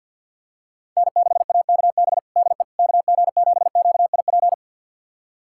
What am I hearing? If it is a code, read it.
Morse code, heard as a keyed tone, about 52 wpm, 700 Hz. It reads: N6AKB DE KG7QIP
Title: USA Callsign DE Callsign 50wpm